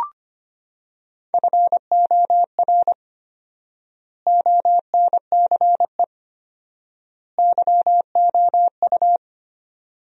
Morse code, heard as a keyed tone, about 25 wpm, 700 Hz. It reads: FOR ONCE YOU
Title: 4x Word-Spacing - Sentences from Top 300 Words - 25wpm